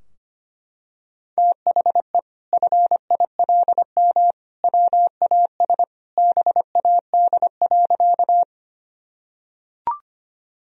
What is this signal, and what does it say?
Morse code, heard as a keyed tone, about 25 wpm, 700 Hz. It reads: THE FILM WAS BAD.
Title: Sentences from Top 1000 Words - Review Part 3 - 25wpm